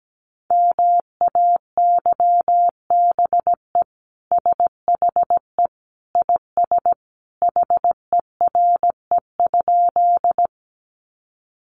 Morse code, heard as a keyed tone, about 17 wpm, 700 Hz. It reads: MAYBE SHE IS HERE?
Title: Sentences from Top 1000 Words 17wpm